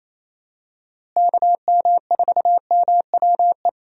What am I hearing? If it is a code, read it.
Morse code, heard as a keyed tone, about 28 wpm, 700 Hz. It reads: KM4MWE
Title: USA Call Signs - 6 Characters - Repeated 3x - 28wpm